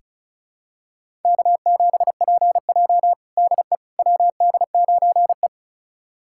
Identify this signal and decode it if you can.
Morse code, heard as a keyed tone, about 34 wpm, 700 Hz. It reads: K7PJ DE WD9E
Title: USA Callsign DE Callsign - 4 Characters - 35wpm